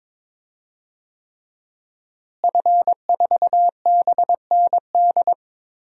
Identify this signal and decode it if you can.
Morse code, heard as a keyed tone, about 21 wpm, 700 Hz. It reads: F4BND
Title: DXCC-100 Call Signs - 5 Characters - Repeated 3x - 22wpm